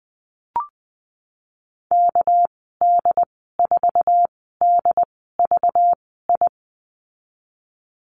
Morse code, heard as a keyed tone, about 20 wpm, spaced as for 15 wpm, 700 Hz. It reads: KD4DVI
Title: 213 Lesson 34g - Number 8 - Call Signs 20wpm